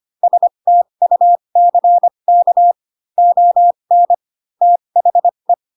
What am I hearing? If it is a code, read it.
Morse code, heard as a keyed tone, about 25 wpm, spaced as for 22 wpm, 700 Hz. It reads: STUCK ON THE